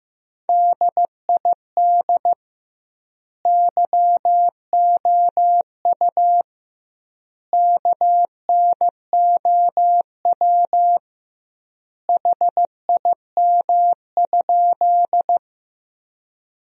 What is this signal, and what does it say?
Morse code, heard as a keyed tone, about 15 wpm, 700 Hz. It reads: DID YOU KNOW HIM?
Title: 2x Word-Spacing - Sentences from Top 100 Words - 15wpm